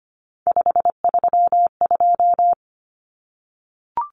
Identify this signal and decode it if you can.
Morse code, heard as a keyed tone, about 25 wpm, 700 Hz. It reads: 532
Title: Numbers - 3-Digit 25wpm